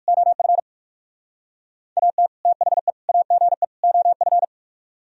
Morse code, heard as a keyed tone, about 45 wpm, 700 Hz. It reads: OF AT THE AGE OF
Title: Speed-Racing - Sets of 4 Words 45wpm